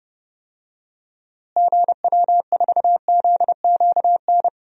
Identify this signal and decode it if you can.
Morse code, heard as a keyed tone, about 30 wpm, 700 Hz. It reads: GW4ZQN
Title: DXCC-100 Call Signs - 6 Characters - Repeated 3x - 30wpm